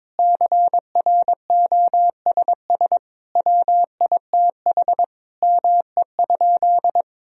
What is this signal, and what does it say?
Morse code, heard as a keyed tone, about 22 wpm, 700 Hz. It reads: CROSS WITH ME?
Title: Sentences from Top 1000 Words - Review Part 1 - 22wpm